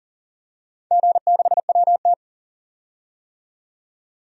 Morse code, heard as a keyed tone, about 40 wpm, 700 Hz. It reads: G6WT
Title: DXCC-100 Call Signs - 4 Characters - 40wpm